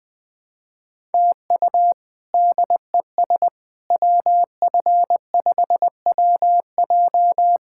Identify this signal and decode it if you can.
Morse code, heard as a keyed tone, about 21 wpm, 700 Hz. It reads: TU DES WF5WJ
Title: K1USN SST QSO Practice - 20wpm